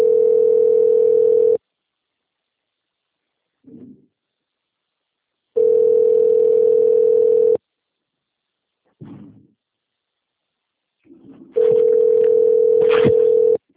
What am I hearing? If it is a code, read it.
Ringback tone (North America), 440 + 480 Hz.